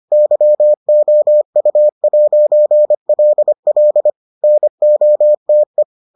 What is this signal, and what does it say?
Morse code, heard as a keyed tone, about 25 wpm, 600 Hz. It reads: YOU'LL NOTE